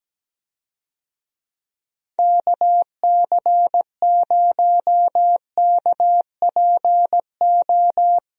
Morse code, heard as a keyed tone, about 17 wpm, 700 Hz. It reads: KC0KPO
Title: USA Call Signs - 6 Characters - Repeated 3x - 17wpm